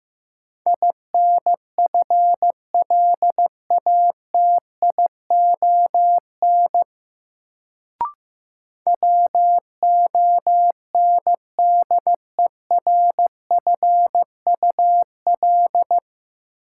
Morse code, heard as a keyed tone, about 15 wpm, 700 Hz. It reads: INFLATION WONDERFUL
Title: Sentences from Top 1100 Words - New Word Drill - 15wpm